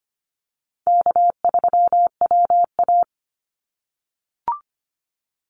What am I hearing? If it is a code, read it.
Morse code, heard as a keyed tone, about 25 wpm, 700 Hz. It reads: K3WA
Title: Easy Call Signs - 25wpm